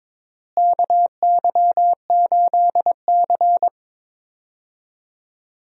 Morse code, heard as a keyed tone, about 22 wpm, 700 Hz. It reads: KY8C
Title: USA Call Signs - 4 Characters - Repeated 3x - 22wpm